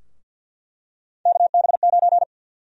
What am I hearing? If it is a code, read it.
Morse code, heard as a keyed tone, about 53 wpm, 700 Hz. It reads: KB9
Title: USA Call Sign Prefixes - Repeated 2x - 50wpm